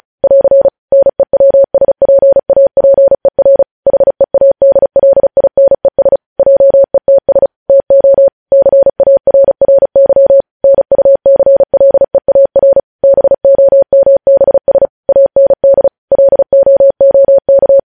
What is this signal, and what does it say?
Morse code, heard as a keyed tone, about 36 wpm, 550 Hz. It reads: <AR> NEWSPAPER HEADLINES JETS TO CARRY NUCLEAR BOMBS AND LOOK
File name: podcast / News headlines-1x-35x35